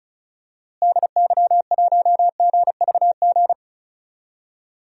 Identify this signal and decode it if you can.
Morse code, heard as a keyed tone, about 36 wpm, 700 Hz. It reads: DY1GVG
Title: DXCC-100 Call Signs - 6 Characters - Repeated 2x - 35wpm